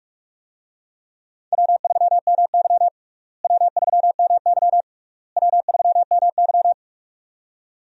Morse code, heard as a keyed tone, about 41 wpm, 700 Hz. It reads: W3MY W3MY W3MY
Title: USA Call Signs - 4 Characters - Repeated 3x - 45wpm